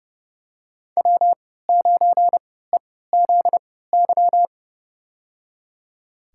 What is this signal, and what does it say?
Morse code, heard as a keyed tone, about 30 wpm, spaced as for 17 wpm, 700 Hz. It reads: W9EZY
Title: USA Call Signs - 5 Characters - 3x Character Spacing - 30wpm